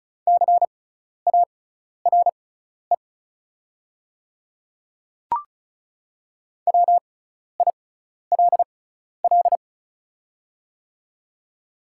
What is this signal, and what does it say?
Morse code, heard as a keyed tone, about 34 wpm, spaced as for 12 wpm, 690 Hz. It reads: CARE WILL
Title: 6x Character Spacing - Top 500 Words - 35wpm